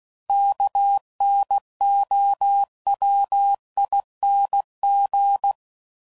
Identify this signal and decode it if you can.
Morse code, heard as a keyed tone, about 16 wpm, 800 Hz. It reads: KNOWING
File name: Suffixes / Special INGs / Sngl Ings